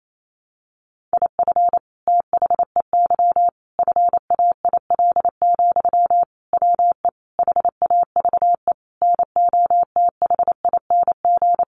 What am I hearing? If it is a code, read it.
Morse code, heard as a keyed tone, about 28 wpm, 700 Hz. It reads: IF THEY FAIL, WE HAVE NOTHING
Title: Sentences from Top 1100 Words 28wpm